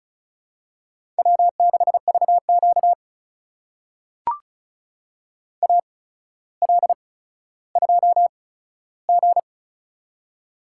Morse code, heard as a keyed tone, about 36 wpm, spaced as for 15 wpm, 700 Hz. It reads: W6VQ AL2G
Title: USA Call Signs - 4 Characters - 8x Character Spacing - 35wpm